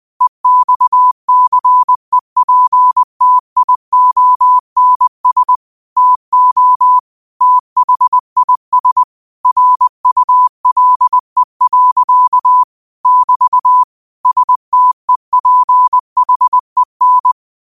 Morse code, heard as a keyed tone, about 20 wpm, 1000 Hz. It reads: EXCEPTIONS TO THIS RULE. <BT> STEPHEN
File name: qotdfiles / 20wpm